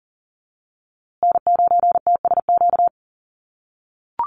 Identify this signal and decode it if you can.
Morse code, heard as a keyed tone, about 40 wpm, 700 Hz. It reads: N9TSQ
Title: USA Call Signs Repeated 2x 40wpm